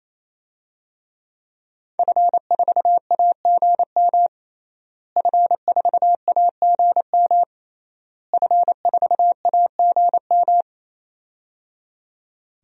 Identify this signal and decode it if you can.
Morse code, heard as a keyed tone, about 28 wpm, 700 Hz. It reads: F4AGM F4AGM F4AGM
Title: DXCC-100 Call Signs - 5 Characters - Repeated 3x - 28wpm